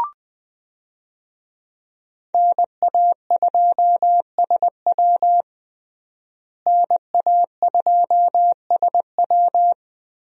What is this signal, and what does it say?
Morse code, heard as a keyed tone, about 20 wpm, 700 Hz. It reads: NA2SW NA2SW
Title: USA Call Signs - 5 Characters - Repeated 3x - 20wpm